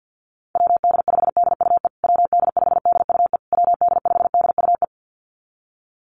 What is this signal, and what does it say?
Morse code, heard as a keyed tone, about 50 wpm, 700 Hz. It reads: WD5DUE WD5DUE WD5DUE
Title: USA Call Signs Repeated 3x 50wpm